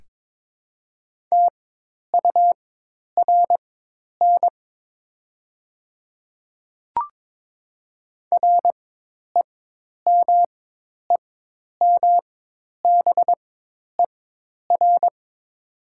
Morse code, heard as a keyed tone, about 22 wpm, spaced as for 10 wpm, 700 Hz. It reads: TURN REMEMBER
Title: 4x Character Spacing - Top 500 Words - 22wpm